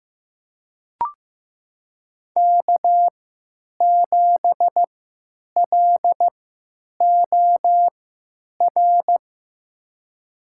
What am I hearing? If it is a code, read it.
Morse code, heard as a keyed tone, about 15 wpm, spaced as for 9 wpm, 700 Hz. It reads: K7LOR
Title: USA Call Signs - 5 Characters - 3x Character Spacing - 15wpm